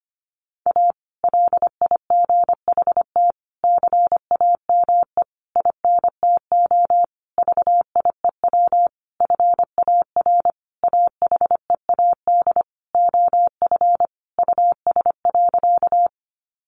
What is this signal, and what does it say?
Morse code, heard as a keyed tone, about 25 wpm, 700 Hz. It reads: A LIGHT CAME INTO VIEW FAR AHEAD OF US.
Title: Sentences from Top 1100 Words 25wpm